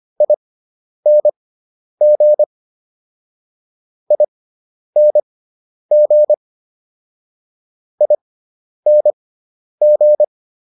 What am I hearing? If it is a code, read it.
Morse code, heard as a keyed tone, about 25 wpm, spaced as for 10 wpm, 600 Hz. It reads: ING ING ING